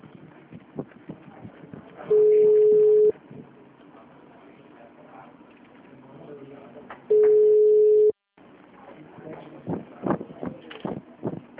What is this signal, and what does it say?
Ringback tone (Europe, 425 Hz), with speech.